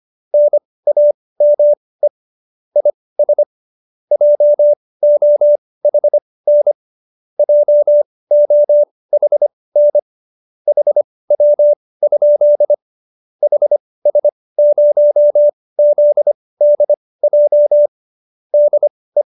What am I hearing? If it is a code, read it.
Morse code, heard as a keyed tone, about 25 wpm, spaced as for 18 wpm, 590 Hz. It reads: NAME IS JOHN JOHN HW? HS0ZDJ DE